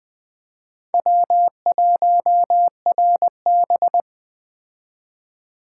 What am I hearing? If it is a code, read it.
Morse code, heard as a keyed tone, about 21 wpm, 700 Hz. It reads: W1RB